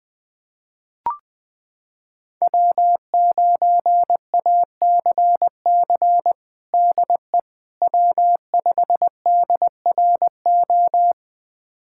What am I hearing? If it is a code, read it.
Morse code, heard as a keyed tone, about 20 wpm, 700 Hz. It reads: W9ACC DE W5DRO
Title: USA Callsign DE Callsign - 5 Characters - 20wpm